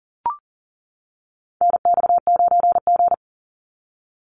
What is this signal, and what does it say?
Morse code, heard as a keyed tone, about 40 wpm, 700 Hz. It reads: NX9G
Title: Call Signs - 40wpm